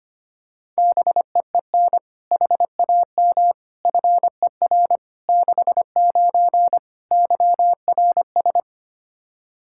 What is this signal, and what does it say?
Morse code, heard as a keyed tone, about 25 wpm, 700 Hz. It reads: BEEN HAM FER 69 YRS
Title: Practice QSO - Exchange 3 - More Info - 25wpm